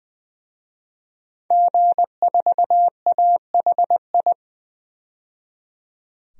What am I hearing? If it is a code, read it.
Morse code, heard as a keyed tone, about 20 wpm, 700 Hz. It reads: G4AHI
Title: DXCC-100 Call Signs - 5 Characters - 20wpm